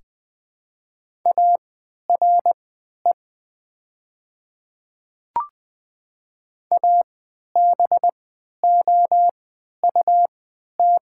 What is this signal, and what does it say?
Morse code, heard as a keyed tone, about 20 wpm, spaced as for 11 wpm, 700 Hz. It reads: ARE ABOUT
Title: 3x Character Spacing - Top 200 Words - 20wpm